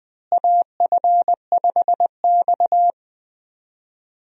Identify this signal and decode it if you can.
Morse code, heard as a keyed tone, about 21 wpm, 700 Hz. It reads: AF5X
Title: USA Call Signs - 4 Characters - Repeated 3x - 20wpm